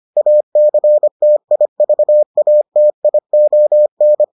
Morse code, heard as a keyed tone, about 25 wpm, 600 Hz. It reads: ACTIVATION